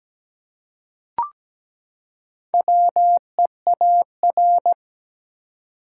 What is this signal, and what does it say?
Morse code, heard as a keyed tone, about 17 wpm, 700 Hz. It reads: WEAR
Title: Four Letter Words - 17wpm